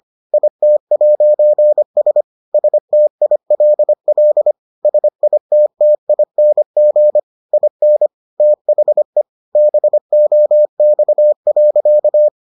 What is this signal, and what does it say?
Morse code, heard as a keyed tone, about 25 wpm, 600 Hz. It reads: IT'S STILL SITTING IN THE BOX.